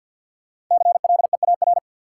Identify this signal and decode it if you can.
Morse code, heard as a keyed tone, about 52 wpm, 700 Hz. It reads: CLEAR
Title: Sets of 5 Words Encore - New Word Drill - 50wpm